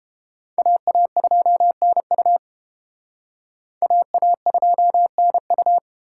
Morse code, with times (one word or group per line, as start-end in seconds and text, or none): 0.6-2.4 AA2NU
3.8-5.8 AA2NU